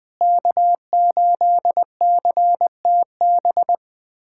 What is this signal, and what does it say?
Morse code, heard as a keyed tone, about 20 wpm, 700 Hz. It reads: K8CTB
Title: USA Callsign DE Callsign - 5 Characters - 20wpm